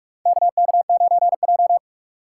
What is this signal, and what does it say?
Morse code, heard as a keyed tone, about 46 wpm, 700 Hz. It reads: KK9J
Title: USA Call Signs - 4 Characters - 6x Character Spacing - 45wpm